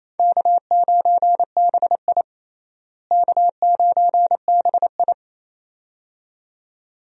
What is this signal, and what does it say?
Morse code, heard as a keyed tone, about 28 wpm, 700 Hz. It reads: K9BI K9BI
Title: USA Call Signs - 4 Characters - Repeated 2x - 28wpm